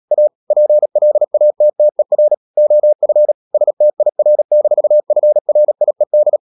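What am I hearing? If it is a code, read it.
Morse code, heard as a keyed tone, about 37 wpm, 600 Hz. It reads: A PLATTER OF STIR-FRIED
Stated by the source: Morse code practice at 35–39 wpm sending English text of Ken Follett's novel Never, from CW med Gnister.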